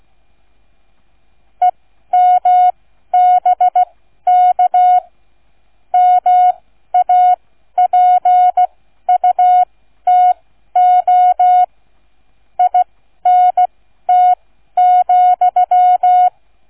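Morse code, heard as a keyed tone, about 16 wpm, spaced as for 12 wpm, 730 Hz. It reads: EMBK MAPUTO INT,
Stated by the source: Morse code receive practice, flight number, from ZL3TK.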